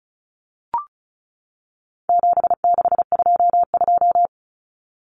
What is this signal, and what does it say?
Morse code, heard as a keyed tone, about 34 wpm, 700 Hz. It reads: 7622
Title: Common Cut Numbers - 4-Digit 35wpm